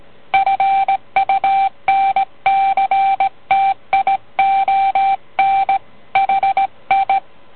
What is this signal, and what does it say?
Morse code, heard as a keyed tone, about 18 wpm, 780 Hz. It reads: FUNCTION HI